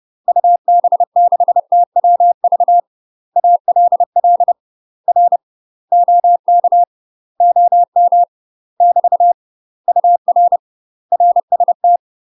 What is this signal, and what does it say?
Morse code, heard as a keyed tone, about 30 wpm, 700 Hz. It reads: UB6TWV ALL R OK OM <BT> UR RST
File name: UK MASTER QSOs TRAINING / 30WPM / UK MASTER QSOs 0005